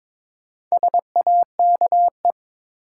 Morse code, heard as a keyed tone, about 22 wpm, 700 Hz. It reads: SAKE
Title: Sets of 4 Words - New Word Drill - 22wpm